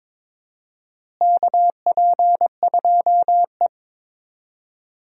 Morse code, heard as a keyed tone, about 22 wpm, 700 Hz. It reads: KP2E
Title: DXCC-100 Call Signs - 4 Characters - Repeated 3x - 22wpm